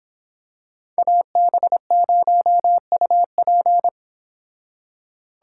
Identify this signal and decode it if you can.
Morse code, heard as a keyed tone, about 25 wpm, 700 Hz. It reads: AB0UP